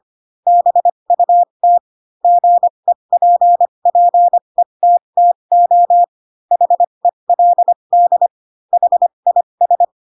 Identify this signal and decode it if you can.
Morse code, heard as a keyed tone, about 25 wpm, spaced as for 22 wpm, 700 Hz. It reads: BUT GEPPETTO HELD HIS
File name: Pinocchio / Pinocchio ch7 18 25 ik1hge